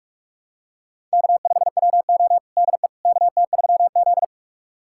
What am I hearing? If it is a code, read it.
Morse code, heard as a keyed tone, about 46 wpm, 700 Hz. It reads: K5WO DE KT3Z
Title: USA Callsign DE Callsign - 4 Characters - 45wpm